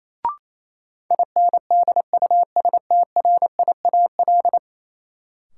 Morse code, heard as a keyed tone, about 28 wpm, 700 Hz. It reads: INDUSTRIAL
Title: Sets of 5 Words - New Word Drill - 28wpm